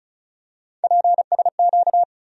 Morse code, heard as a keyed tone, about 36 wpm, 700 Hz. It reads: PSQ
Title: USA Call Sign Postfixes - 3 Letters 35wpm